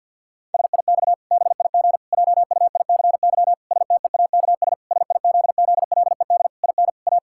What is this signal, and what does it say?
Morse code, heard as a keyed tone, about 50 wpm, 700 Hz. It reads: SIX BIG JUICY STEAKS SIZZLED IN A
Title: Pangram Challenge v2 - 50wpm